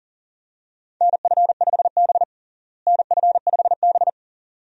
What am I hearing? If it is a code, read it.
Morse code, heard as a keyed tone, about 40 wpm, 700 Hz. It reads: NF5B NF5B
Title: USA Call Signs - 4 Characters - Repeated 2x - 40wpm